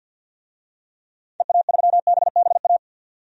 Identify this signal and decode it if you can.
Morse code, heard as a keyed tone, about 52 wpm, 700 Hz. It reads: EA3BBA
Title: DXCC-100 Call Signs - 6 Characters - 50wpm